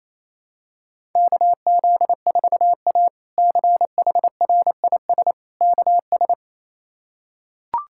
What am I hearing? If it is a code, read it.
Morse code, heard as a keyed tone, about 27 wpm, 700 Hz. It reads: KZ4A CHRIS KS
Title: K1USN SST QSO Practice - 28wpm